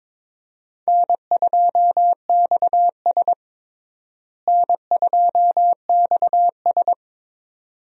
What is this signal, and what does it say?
Morse code, heard as a keyed tone, about 22 wpm, 700 Hz. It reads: N2XS N2XS
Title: USA Call Signs - 4 Characters - Repeated 2x - 22wpm